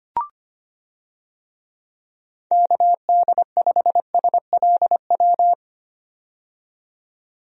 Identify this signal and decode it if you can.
Morse code, heard as a keyed tone, about 25 wpm, 700 Hz. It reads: KD5SLW
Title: USA Call Signs - 6 Characters - 25wpm